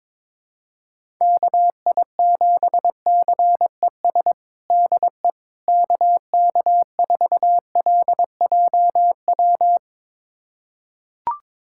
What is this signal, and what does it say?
Morse code, heard as a keyed tone, about 22 wpm, 700 Hz. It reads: KI7CES DE KK4LJW
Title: USA Callsign DE Callsign 22wpm